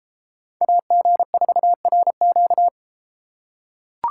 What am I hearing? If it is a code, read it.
Morse code, heard as a keyed tone, about 30 wpm, 700 Hz. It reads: AG4RQ